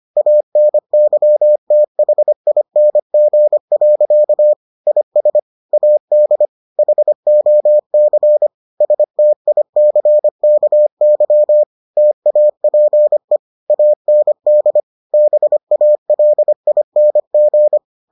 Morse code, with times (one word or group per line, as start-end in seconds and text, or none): 0.2-4.5 ANYTHING.
4.9-5.4 IS
5.7-6.5 AD
6.8-8.5 HOC
8.8-11.6 STICKY
12.0-13.4 TAPE
13.7-14.8 AND
15.1-17.8 BALING